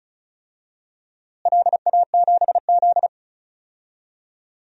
Morse code, heard as a keyed tone, about 34 wpm, 700 Hz. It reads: LA7Z
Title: DXCC-100 Call Signs - 4 Characters - Repeated 3x - 35wpm